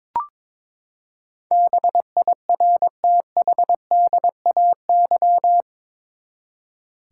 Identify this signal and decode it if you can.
Morse code, heard as a keyed tone, about 22 wpm, 700 Hz. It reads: BIRTHDAY
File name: practice / 22wpm / Eight Letter Words - 22wpm